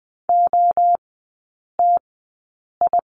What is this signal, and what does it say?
Morse code, heard as a keyed tone, about 20 wpm, spaced as for 8 wpm, 700 Hz. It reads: OTI
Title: 035 Lesson 7e - Letter s - Three Characters 20wpm